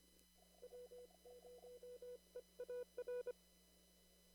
Morse code, heard as a keyed tone, about 24 wpm, 490 Hz. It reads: W0EAR
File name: CW / kt5j wpx cw-op-k5pi